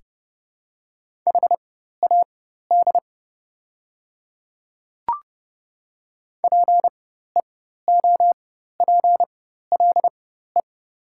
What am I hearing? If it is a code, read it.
Morse code, heard as a keyed tone, about 29 wpm, spaced as for 14 wpm, 690 Hz. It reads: HAD PEOPLE
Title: 4x Character Spacing - Top 200 Words - 30wpm